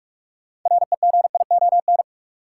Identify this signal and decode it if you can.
Morse code, heard as a keyed tone, about 44 wpm, 700 Hz. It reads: REGION
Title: Sentences from Top 1500 Words - New Word Drill Alt - 45wpm